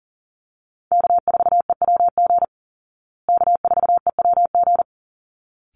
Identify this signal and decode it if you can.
Morse code, heard as a keyed tone, about 40 wpm, 700 Hz. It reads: K4EWG K4EWG
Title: Call Signs Repeated 3x - 40wpm